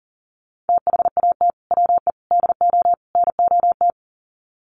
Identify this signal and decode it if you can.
Morse code, heard as a keyed tone, about 40 wpm, 700 Hz. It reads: THAT WE DO NOT
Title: Sets of 4 Words 40wpm